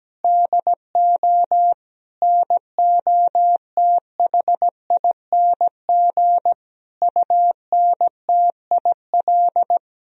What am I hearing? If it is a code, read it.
Morse code, heard as a keyed tone, about 17 wpm, 700 Hz. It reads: DO NOTHING UNTIL
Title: Sentences from Top 500 Words and Abbreviations 17wpm